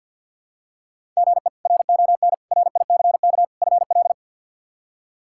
Morse code, heard as a keyed tone, about 48 wpm, 700 Hz. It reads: GE RON RICK FL
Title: K1USN SST QSO Practice - 50wpm